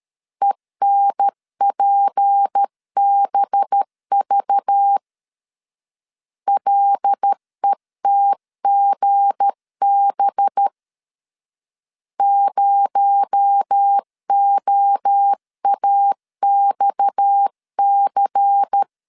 Morse code, heard as a keyed tone, about 13 wpm, 800 Hz. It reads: ENPBV LETGB 0OAXC
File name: LEZIONI / 7 GF9Z / 7 G 12wpm 2055